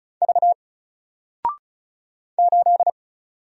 Morse code, heard as a keyed tone, about 34 wpm, 700 Hz. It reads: V 8
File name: Single-Letter-Number-35wpm